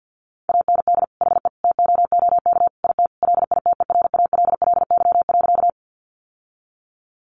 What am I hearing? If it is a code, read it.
Morse code, heard as a keyed tone, about 50 wpm, 700 Hz. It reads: AND HE TOOK IT LITERALLY.
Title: Sentences from Top 1400 Words 50wpm